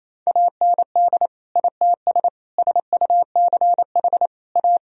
Morse code, heard as a keyed tone, about 28 wpm, 700 Hz. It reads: AND ITS SUCH A
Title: Sentences from Top 1000 Words - Review Part 2 - 28wpm